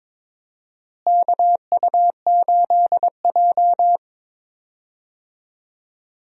Morse code, heard as a keyed tone, about 22 wpm, 700 Hz. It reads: KU8J
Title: USA Call Signs - 4 Characters - Repeated 2x - 22wpm